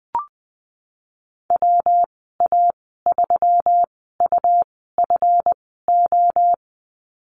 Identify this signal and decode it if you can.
Morse code, heard as a keyed tone, about 20 wpm, spaced as for 15 wpm, 700 Hz. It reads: WA3UFO
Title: 186 Lesson 30g - Letter b - Call Signs 20wpm